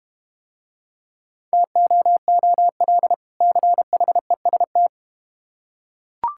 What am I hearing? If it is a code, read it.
Morse code, heard as a keyed tone, about 31 wpm, 690 Hz. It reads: TOOL CHEST
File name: practice / 25wpm / Difficult Words - Speed-Racing with Hints - 25wpm